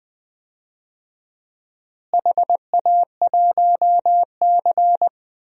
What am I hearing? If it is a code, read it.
Morse code, heard as a keyed tone, about 21 wpm, 700 Hz. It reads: HA1C
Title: DXCC-100 Call Signs - 4 Characters - Repeated 3x - 20wpm